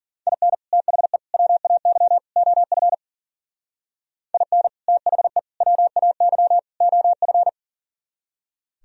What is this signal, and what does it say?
Morse code, heard as a keyed tone, about 50 wpm, 700 Hz. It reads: IN THE WAY OF IN THE WAY OF